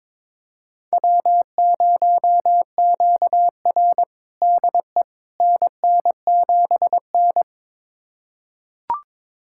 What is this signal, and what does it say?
Morse code, heard as a keyed tone, about 22 wpm, 700 Hz. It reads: W0QR DE NN7N
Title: USA Callsign DE Callsign - 4 Characters - 22wpm